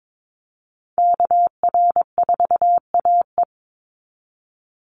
Morse code, heard as a keyed tone, about 22 wpm, 700 Hz. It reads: KR4AE
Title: Call Signs Repeated 2x - 22wpm